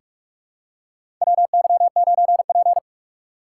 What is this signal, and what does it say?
Morse code, heard as a keyed tone, about 44 wpm, 700 Hz. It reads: WY9P
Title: USA Call Signs - 4 Characters - 4x Character Spacing - 45wpm